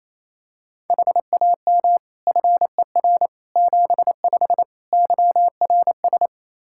Morse code, heard as a keyed tone, about 28 wpm, 700 Hz. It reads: HAM FER 75 YRS
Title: K1USN SST QSO Practice 28wpm